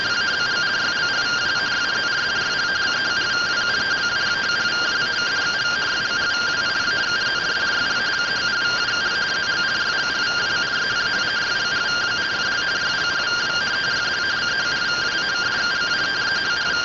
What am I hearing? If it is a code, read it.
Two-tone teleprinter signal, about 170 Hz shift.